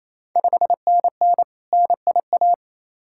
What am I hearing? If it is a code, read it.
Morse code, heard as a keyed tone, about 28 wpm, 690 Hz. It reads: 5NN NIA
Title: Ontario QSO Party Exchange 28wpm